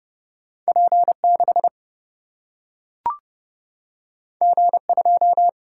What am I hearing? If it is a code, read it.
Morse code, heard as a keyed tone, about 30 wpm, 700 Hz. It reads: P6 G2
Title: Single-Letter Single-Number 30wpm